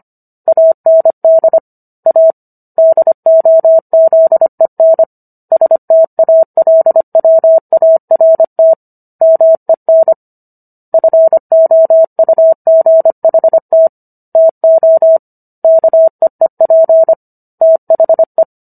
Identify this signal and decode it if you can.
Morse code, heard as a keyed tone, about 25 wpm, 650 Hz. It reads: AND A DOZEN STALWART MEN FOUGHT TO KEEP THE